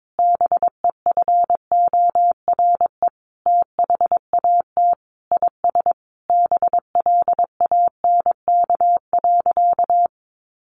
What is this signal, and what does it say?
Morse code, heard as a keyed tone, about 22 wpm, 700 Hz. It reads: BEFORE THAT IS BLANK.
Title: Sentences from Top 1500 Words 22wpm